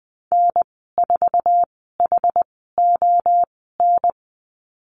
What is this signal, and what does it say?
Morse code, heard as a keyed tone, about 20 wpm, spaced as for 15 wpm, 700 Hz. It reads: N4HON
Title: 076 Lesson 13g - Letter L - Call Signs 20wpm